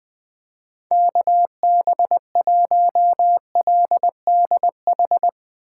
Morse code, heard as a keyed tone, about 20 wpm, 700 Hz. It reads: KB1LDH
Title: Speed Racing - USA 2x3 Call Signs 20wpm